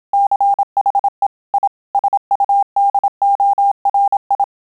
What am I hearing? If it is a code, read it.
Morse code, heard as a keyed tone, about 27 wpm, 800 Hz. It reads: CHE I SUDORI